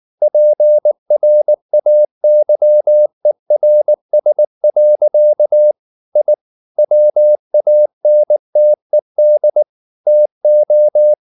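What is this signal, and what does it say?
Morse code, heard as a keyed tone, about 19 wpm, 600 Hz. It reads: PRAYERS. I WANTED TO
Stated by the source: Morse code at 19 wpm from CW med Gnister, reading Ken Follett's novel Never in English.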